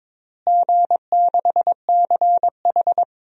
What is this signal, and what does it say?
Morse code, heard as a keyed tone, about 22 wpm, 700 Hz. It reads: G6CH
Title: DXCC-100 Call Signs - 4 Characters - Repeated 2x - 22wpm